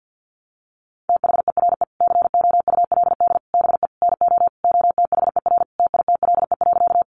Morse code, heard as a keyed tone, about 50 wpm, 700 Hz. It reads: THERE COULD BE NO OTHER TITLE.
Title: Sentences from Top 1200 Words 50wpm